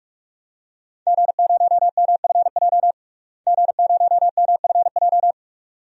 Morse code, heard as a keyed tone, about 46 wpm, 700 Hz. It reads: G0MFJ G0MFJ
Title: DXCC-100 Call Signs - 5 Characters - Repeated 2x - 45wpm